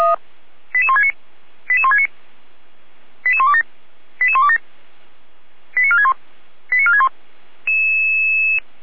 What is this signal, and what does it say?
Five-tone selective call (ZVEI), heard as quick runs of five tones, tones of about 70 ms. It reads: 88169 88169 88127 88127 89461 89461